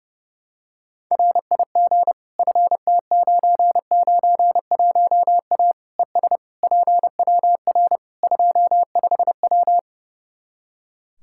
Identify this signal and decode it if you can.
Morse code, heard as a keyed tone, about 30 wpm, 700 Hz. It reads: RIG FT991A ES PWR 25W
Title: Practice QSO - Exchange 2 - Info - 30wpm